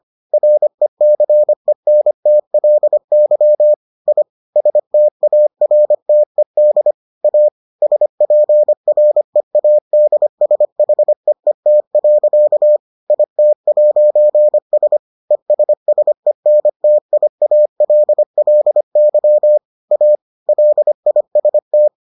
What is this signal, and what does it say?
Morse code, heard as a keyed tone, about 25 wpm, 600 Hz. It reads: RECENTLY I STARTED A SPREADSHEET. IT'S ESSENTIALLY A LIST